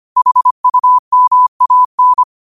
Morse code, heard as a keyed tone, about 25 wpm, 1000 Hz. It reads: HUMAN
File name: qotdfiles / 25wpm